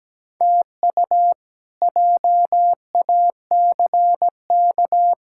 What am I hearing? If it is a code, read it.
Morse code, heard as a keyed tone, about 17 wpm, 700 Hz. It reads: TU JACK
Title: K1USN SST QSO Practice - 17wpm